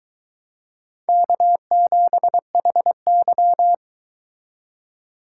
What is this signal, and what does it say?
Morse code, heard as a keyed tone, about 22 wpm, 700 Hz. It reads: K7HY